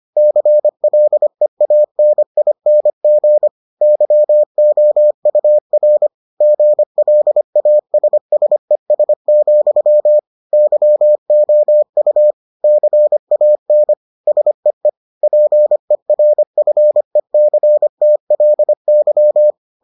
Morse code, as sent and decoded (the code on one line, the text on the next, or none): -.-. .-.. .- -. .. -. --. / -.-- --- ..- .-. / --. .-.. .- ... ... ... --..-- / -.-- --- ..- / -.-. .- -. / ... . / .--. .-. ..-. -.-. - .-.. -.--
CLEANING YOUR GLASSES, YOU CAN SEE PERFECTLY